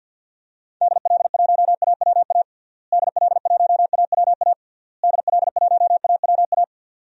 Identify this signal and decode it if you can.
Morse code, heard as a keyed tone, about 49 wpm, 700 Hz. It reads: DL1AWA DL1AWA DL1AWA
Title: DXCC-100 Call Signs - 6 Characters - Repeated 3x - 50wpm